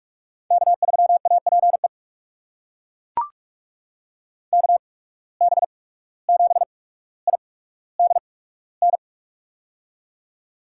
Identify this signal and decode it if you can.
Morse code, heard as a keyed tone, about 42 wpm, spaced as for 18 wpm, 700 Hz. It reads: K3APE KB7IDN
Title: USA Call Signs - 8x Character Spacing - 45wpm